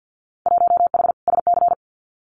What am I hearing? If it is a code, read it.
Morse code, heard as a keyed tone, about 50 wpm, 700 Hz. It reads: 1H SC
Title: Winter Field Day Exchanges 50wpm